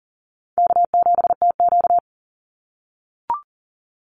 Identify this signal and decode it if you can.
Morse code, heard as a keyed tone, about 39 wpm, 700 Hz. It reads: K7TQ
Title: NA SOTA Chaser Callsigns 40wpm